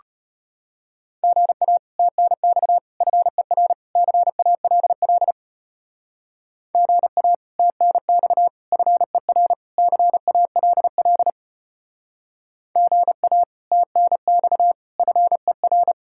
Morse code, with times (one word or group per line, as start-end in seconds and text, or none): 1.2-1.8 GA
2.0-2.8 TNX
3.0-3.7 FER
4.0-5.3 CALL
6.8-7.3 GA
7.6-8.5 TNX
8.7-9.5 FER
9.8-11.3 CALL
12.8-13.4 GA
13.7-14.7 TNX
15.0-15.9 FER